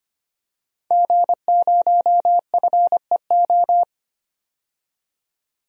Morse code, heard as a keyed tone, about 25 wpm, 700 Hz. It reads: G0FEO
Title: DXCC-100 Call Signs - 5 Characters - 25wpm